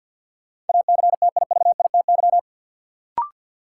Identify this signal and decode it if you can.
Morse code, heard as a keyed tone, about 51 wpm, 700 Hz. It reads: ACTIVITY
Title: Sentences from Top 1000 Words - Review Part 4 - New Word Drill - 50wpm